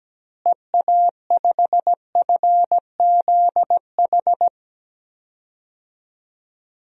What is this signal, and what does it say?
Morse code, heard as a keyed tone, about 17 wpm, 700 Hz. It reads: EA5FZH
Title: DXCC-100 Call Signs - 6 Characters - Repeated 2x - 17wpm